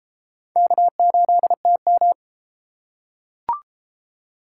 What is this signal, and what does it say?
Morse code, heard as a keyed tone, about 30 wpm, 700 Hz. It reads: K8TM